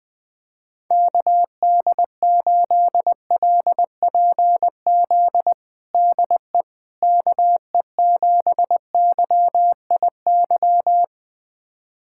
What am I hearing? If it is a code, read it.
Morse code, heard as a keyed tone, about 20 wpm, 700 Hz. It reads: KD8LPZ DE KE7YIY
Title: USA Callsign DE Callsign - 6 Characters - 20wpm